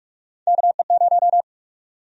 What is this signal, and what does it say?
Morse code, heard as a keyed tone, about 46 wpm, 700 Hz. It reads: KE0
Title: USA Call Sign Prefixes - 45wpm